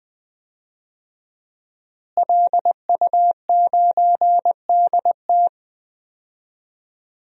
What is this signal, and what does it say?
Morse code, heard as a keyed tone, about 20 wpm, 700 Hz. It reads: LU9DT
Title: DXCC-100 Call Signs - 5 Characters - Repeated 2x - 20wpm